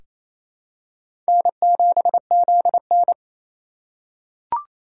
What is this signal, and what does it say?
Morse code, heard as a keyed tone, about 28 wpm, 700 Hz. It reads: N7ZN
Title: K1USN SST QSO Practice - 28wpm